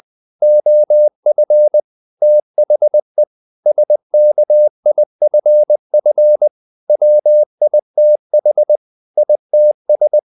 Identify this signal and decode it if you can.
Morse code, heard as a keyed tone, about 20 wpm, 600 Hz. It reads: OF THE SKIFF WITH ITS
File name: the old man and the sea part 2 20wpm